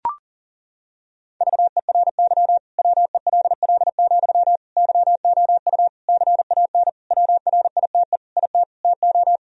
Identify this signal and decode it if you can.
Morse code, heard as a keyed tone, about 40 wpm, 700 Hz. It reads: VERY WELL, YOU CAN WRITE IT TO
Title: Sentences from Top 1000 Words - Review Part 4 - 40wpm